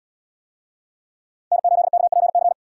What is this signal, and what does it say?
Morse code, heard as a keyed tone, about 75 wpm, 700 Hz. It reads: N8KPC